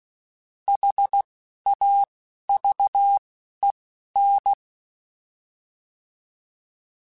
Morse code, heard as a keyed tone, about 16 wpm, spaced as for 11 wpm, 800 Hz. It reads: HAVEN